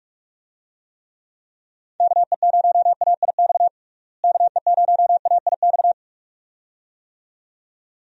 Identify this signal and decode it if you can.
Morse code, heard as a keyed tone, about 46 wpm, 700 Hz. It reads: KE0AIX KE0AIX
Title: USA Call Signs - 6 Characters - Repeated 2x - 45wpm